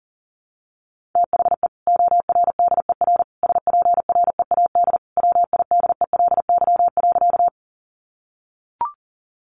Morse code, heard as a keyed tone, about 37 wpm, 700 Hz. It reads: THE ORDER SPREAD WIDELY.
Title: Sentences from Top 1200 Words 40wpm